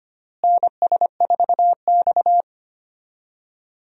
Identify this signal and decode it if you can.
Morse code, heard as a keyed tone, about 25 wpm, 700 Hz. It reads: NS4X
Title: USA Call Signs - 4 Characters - Repeated 3x - 25wpm